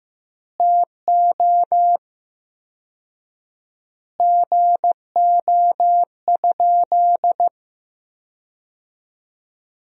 Morse code, heard as a keyed tone, about 15 wpm, 700 Hz. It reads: TO GO?
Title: 4x Word-Spacing - Sentences from Top 300 Words - 15wpm